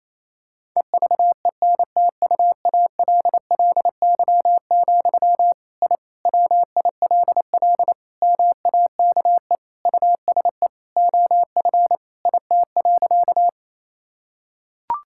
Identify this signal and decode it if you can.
Morse code, heard as a keyed tone, about 28 wpm, 700 Hz. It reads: EVENTUALLY, I WILL MAKE USE OF IT.
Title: Sentences from Top 1000 Words - Review Part 6 - 28wpm